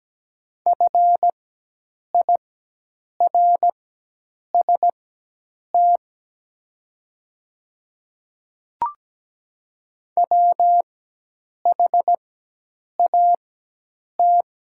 Morse code, heard as a keyed tone, about 17 wpm, spaced as for 8 wpm, 700 Hz. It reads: FIRST WHAT
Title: 4x Character Spacing - Top 100 Words - 17wpm